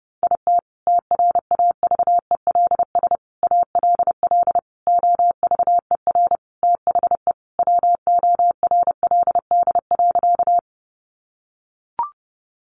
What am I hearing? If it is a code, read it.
Morse code, heard as a keyed tone, about 30 wpm, 700 Hz. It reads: IT TRAVELS ALL OVER THE WORLD.
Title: Additional Easy Phrases 30wpm